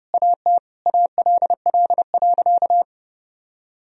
Morse code, heard as a keyed tone, about 30 wpm, 700 Hz. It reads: AT ALL.
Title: Sentences from Top 1000 Words - Review Part 1 - 30wpm